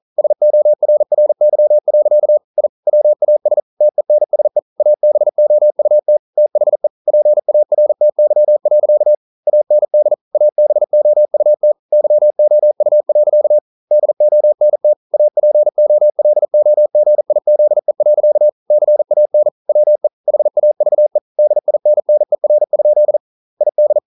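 Morse code, heard as a keyed tone, about 42 wpm, 600 Hz. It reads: SORRY. I WAS TENSE ABOUT THE PARTY. AND ABOUT YOU. DONT APOLOGIZE. CAN WE HAVE DINNER? ID
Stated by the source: Morse code practice at 41 wpm, from CW med Gnister.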